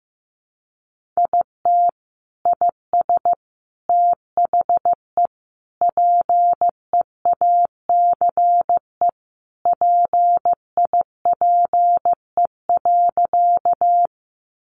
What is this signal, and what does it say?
Morse code, heard as a keyed tone, about 15 wpm, 700 Hz. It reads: IT IS THE PEACE PIPE.
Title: Sentences from Top 1500 Words 15wpm